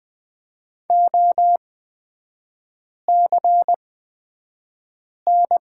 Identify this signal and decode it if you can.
Morse code, heard as a keyed tone, about 20 wpm, spaced as for 5 wpm, 700 Hz. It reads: OCN
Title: 250 Lesson 40e - BK - Three Characters 20wpm